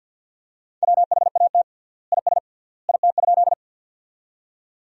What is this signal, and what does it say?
Morse code, heard as a keyed tone, about 50 wpm, 700 Hz. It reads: WHAT IS IT?
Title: 3x Word-Spacing - Sentences from Top 100 Words - 50wpm